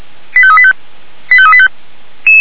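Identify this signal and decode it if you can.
Five-tone selective call (ZVEI), heard as quick runs of five tones, tones of about 70 ms. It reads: 85385 85385
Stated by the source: ZVEI five-tone call 89123 received by SDR, feuerwehr-orlamuende fire brigade.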